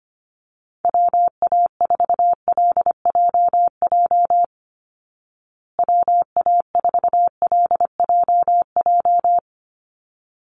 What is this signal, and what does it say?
Morse code, heard as a keyed tone, about 25 wpm, 700 Hz. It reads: WA4LJJ WA4LJJ
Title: Call Signs Repeated 3x - 25wpm